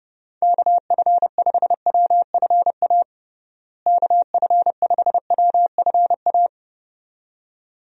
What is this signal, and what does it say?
Morse code, heard as a keyed tone, about 30 wpm, 700 Hz. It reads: KF5WFA KF5WFA
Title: USA Call Signs - 6 Characters - Repeated 2x - 30wpm